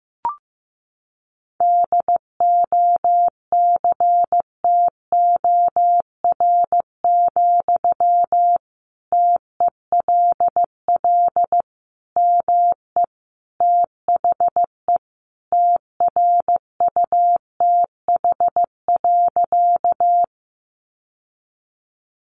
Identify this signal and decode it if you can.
Morse code, heard as a keyed tone, about 15 wpm, 700 Hz. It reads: DOCTOR, TELL ME THE TRUTH.
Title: Sentences from Top 1500 Words 15wpm